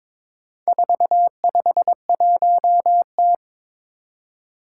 Morse code, heard as a keyed tone, about 22 wpm, 700 Hz. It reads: 451T
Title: Four Challenging Letter-Number Combinations 22wpm